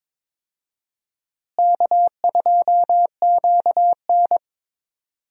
Morse code, heard as a keyed tone, about 21 wpm, 700 Hz. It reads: K2QN
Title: USA Call Signs - 4 Characters - Repeated 2x - 22wpm